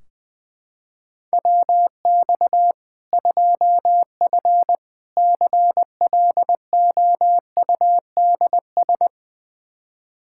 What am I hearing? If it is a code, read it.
Morse code, heard as a keyed tone, about 21 wpm, 700 Hz. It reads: WX 2F CLOUDS
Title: Practice QSO - Exchange 2 - Info - 20wpm